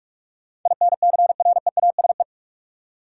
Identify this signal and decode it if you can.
Morse code, heard as a keyed tone, about 46 wpm, 700 Hz. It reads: INCREASE
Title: Sentences from Top 1200 Words - New Word Drill Alt - 45wpm